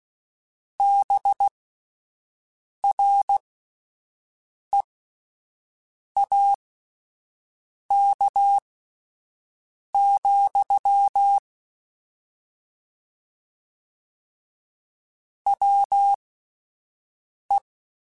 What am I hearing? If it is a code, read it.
Morse code, heard as a keyed tone, about 16 wpm, spaced as for 5 wpm, 800 Hz. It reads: BREAK, WE
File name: (spaced) / QSOs Spaced-mp3s / WA7QOX 1 @10 sp